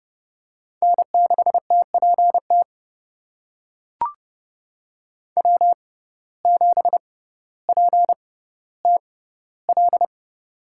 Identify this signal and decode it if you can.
Morse code, heard as a keyed tone, about 32 wpm, spaced as for 16 wpm, 700 Hz. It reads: N6TPT W7PTL
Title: USA Call Signs - 6x Character Spacing - 30wpm